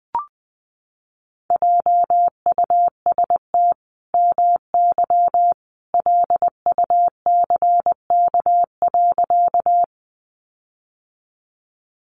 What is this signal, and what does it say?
Morse code, heard as a keyed tone, about 20 wpm, 700 Hz. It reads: JUST MY LUCK.
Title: Sentences from Top 1500 Words 20wpm